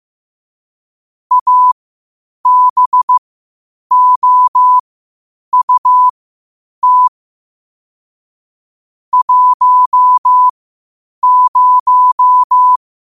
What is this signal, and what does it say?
Morse code, heard as a keyed tone, about 15 wpm, spaced as for 8 wpm, 1000 Hz. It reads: ABOUT 10
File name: qotdfiles / 8wpm